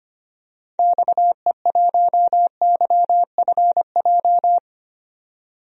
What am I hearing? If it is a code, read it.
Morse code, heard as a keyed tone, about 25 wpm, 700 Hz. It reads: XE1YFJ
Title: DXCC-100 Call Signs - 6 Characters - 25wpm